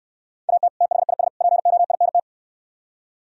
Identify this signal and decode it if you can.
Morse code, heard as a keyed tone, about 68 wpm, 700 Hz. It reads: AT THIS POINT